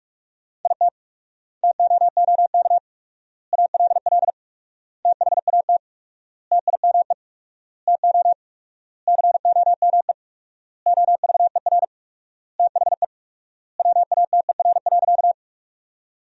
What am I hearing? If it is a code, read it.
Morse code, heard as a keyed tone, about 41 wpm, 700 Hz. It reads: IT TOOK ALL THAT TIME TO COME OVER THE WATER.
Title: 4x Word-Spacing - Sentences from Top 300 Words - 45wpm